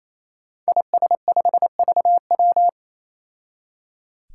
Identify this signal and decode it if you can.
Morse code, heard as a keyed tone, about 28 wpm, 700 Hz. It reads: IS5VW
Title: Five Challenging Letter-Number Combinations 28wpm